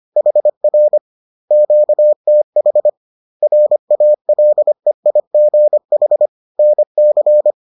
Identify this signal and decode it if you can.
Morse code, heard as a keyed tone, about 25 wpm, 600 Hz. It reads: HR QTH RALEIGH NC